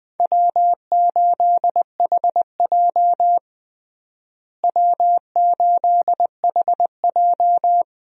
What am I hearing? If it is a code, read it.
Morse code, heard as a keyed tone, about 20 wpm, 700 Hz. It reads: W8HJ W8HJ
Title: USA Call Signs - 4 Characters - Repeated 2x - 20wpm